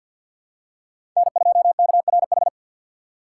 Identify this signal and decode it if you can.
Morse code, heard as a keyed tone, about 48 wpm, 700 Hz. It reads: N2KRH
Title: USA Call Signs - 5 Characters - 2x Character Spacing - 50wpm